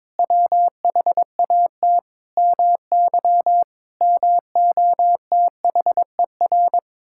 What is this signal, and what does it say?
Morse code, heard as a keyed tone, about 22 wpm, 700 Hz. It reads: WHAT MY MOTHER
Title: Sentences from Top 1000 Words - Review Part 6 - 22wpm